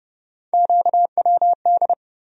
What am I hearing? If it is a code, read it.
Morse code, heard as a keyed tone, about 30 wpm, 700 Hz. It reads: QWD
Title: USA Call Sign Postfixes - 3 Letters 30wpm